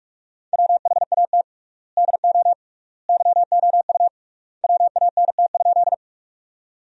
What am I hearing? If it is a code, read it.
Morse code, heard as a keyed tone, about 47 wpm, 700 Hz. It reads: WHAT DO YOU WANT?
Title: 3x Word-Spacing - Sentences from Top 200 Words - 45wpm